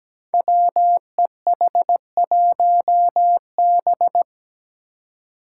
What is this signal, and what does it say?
Morse code, heard as a keyed tone, about 17 wpm, 700 Hz. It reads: WEH1B
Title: Five Challenging Letter-Number Combinations 17wpm